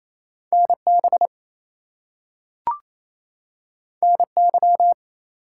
Morse code, heard as a keyed tone, about 28 wpm, 700 Hz. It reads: NB NY
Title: US States and Canadian Provinces 28wpm